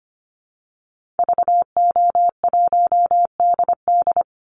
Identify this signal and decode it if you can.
Morse code, heard as a keyed tone, about 25 wpm, 700 Hz. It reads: VO1DD
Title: Call Signs - 25wpm